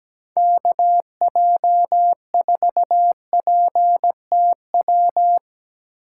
Morse code, heard as a keyed tone, about 17 wpm, 700 Hz. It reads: KJ4PTW
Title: USA Call Signs - 6 Characters - 3x Character Spacing - 17wpm